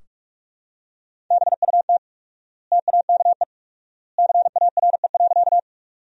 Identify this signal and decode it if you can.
Morse code, heard as a keyed tone, about 46 wpm, 700 Hz. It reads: BUT TAKE CARE.
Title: 4x Word-Spacing - Sentences from Top 300 Words - 45wpm